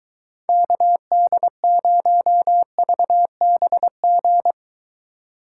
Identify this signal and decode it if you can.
Morse code, heard as a keyed tone, about 23 wpm, 700 Hz. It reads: KD0VBG